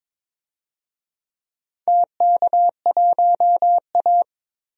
Morse code, heard as a keyed tone, about 22 wpm, 700 Hz. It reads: TK1A
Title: DXCC-100 Call Signs - 4 Characters - 22wpm